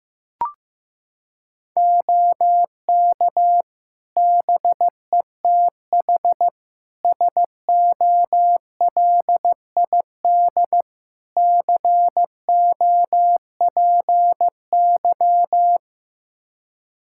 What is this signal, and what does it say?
Morse code, heard as a keyed tone, about 15 wpm, 700 Hz. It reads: OK BETH SOLID COPY
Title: Practice QSO - Exchange 3 - More Info - 15wpm